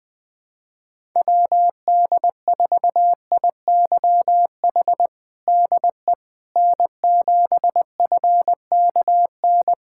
Morse code, heard as a keyed tone, about 20 wpm, 700 Hz. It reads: WD4IYH DE N7FKN
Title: Practice QSO - Exchange 4 - Ending - 20wpm